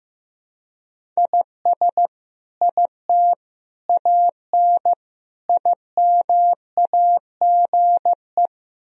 Morse code, heard as a keyed tone, about 15 wpm, 700 Hz. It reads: IS IT AN IMAGE
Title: Sentences from Top 1000 Words - Review Part 4 - 15wpm